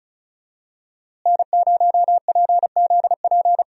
Morse code, heard as a keyed tone, about 36 wpm, 700 Hz. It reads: N0PZP
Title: USA Call Signs - 5 Characters - Repeated 3x - 35wpm